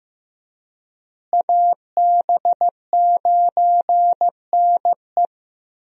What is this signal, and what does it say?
Morse code, heard as a keyed tone, about 16 wpm, 700 Hz. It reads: AB9NE
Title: USA Call Signs - 5 Characters - Repeated 3x - 15wpm